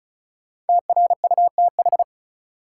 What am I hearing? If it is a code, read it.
Morse code, heard as a keyed tone, about 36 wpm, 700 Hz. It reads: TRUTH
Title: Sentences from Top 800 Words - New Word Drill - 35wpm